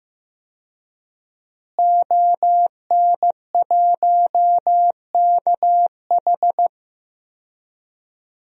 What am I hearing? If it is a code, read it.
Morse code, heard as a keyed tone, about 16 wpm, 700 Hz. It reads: ON1KH
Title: DXCC-100 Call Signs - 5 Characters - Repeated 3x - 15wpm